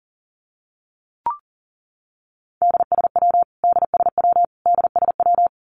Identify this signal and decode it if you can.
Morse code, heard as a keyed tone, about 40 wpm, 700 Hz. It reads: DSW DSW DSW
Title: Abbreviations Repeated 3x - 40wpm